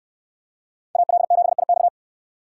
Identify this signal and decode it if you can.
Morse code, heard as a keyed tone, about 69 wpm, 700 Hz. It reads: AL7IX